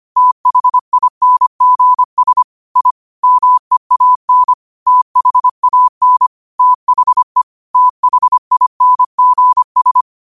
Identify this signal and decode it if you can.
Morse code, heard as a keyed tone, about 25 wpm, 1000 Hz. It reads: THINGS I MEAN THAN THE THINGS